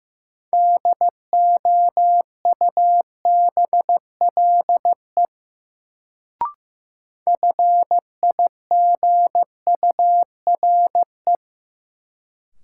Morse code, heard as a keyed tone, about 15 wpm, 700 Hz. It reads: DOUBLE FIGURE
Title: Sentences from Top 1400 Words - New Word Drill Alt - 15wpm